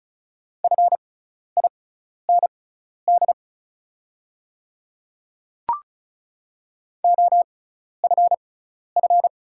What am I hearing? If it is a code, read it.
Morse code, heard as a keyed tone, about 36 wpm, spaced as for 12 wpm, 690 Hz. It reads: FIND OFF
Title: 6x Character Spacing - Top 300 Words - 35wpm